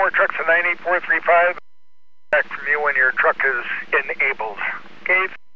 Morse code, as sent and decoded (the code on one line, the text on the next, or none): none